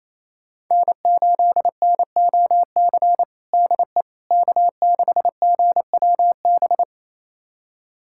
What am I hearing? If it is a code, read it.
Morse code, heard as a keyed tone, about 28 wpm, 700 Hz. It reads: N8NOC DE K6GWB
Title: USA Callsign DE Callsign - 5 Characters - 28wpm